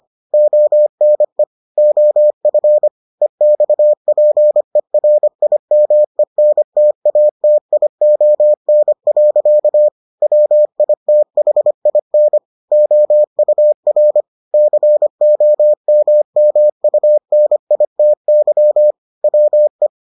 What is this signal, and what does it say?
Morse code, heard as a keyed tone, about 25 wpm, 600 Hz. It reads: ONE OF EXPERIMENTATION. WITHIN OUR COMMUNITY WE